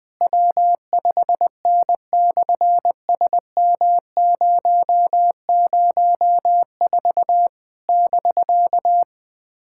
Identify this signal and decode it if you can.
Morse code, heard as a keyed tone, about 20 wpm, 700 Hz. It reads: W5N/SM004 <BK>
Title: SOTA QSO Practice 20wpm